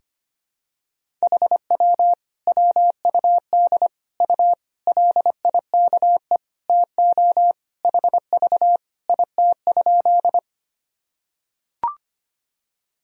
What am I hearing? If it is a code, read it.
Morse code, heard as a keyed tone, about 25 wpm, 700 Hz. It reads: HW WUD U LIKE TO HV IT?
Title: Sentences from Top 100 Words and Abbreviations 25wpm